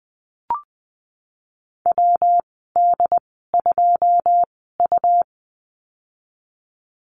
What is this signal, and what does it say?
Morse code, heard as a keyed tone, about 20 wpm, spaced as for 15 wpm, 700 Hz. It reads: WD2U
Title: 154 Lesson 25g - Letter p - Call Signs 20wpm